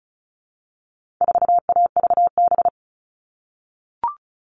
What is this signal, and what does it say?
Morse code, heard as a keyed tone, about 34 wpm, 700 Hz. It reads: 4AVB
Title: All Cut Numbers - 4-Digit 35wpm